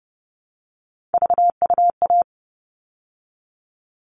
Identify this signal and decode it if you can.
Morse code, heard as a keyed tone, about 30 wpm, 700 Hz. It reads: VUA
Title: All Cut Numbers - 3-Digit 30wpm